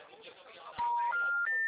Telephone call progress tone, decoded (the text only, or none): special information tone